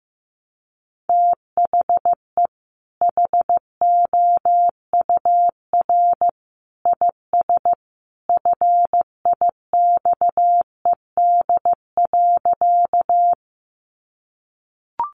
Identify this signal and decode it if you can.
Morse code, heard as a keyed tone, about 15 wpm, 700 Hz. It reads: THE HOUR IS FIXED.
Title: Sentences from Top 1100 Words 15wpm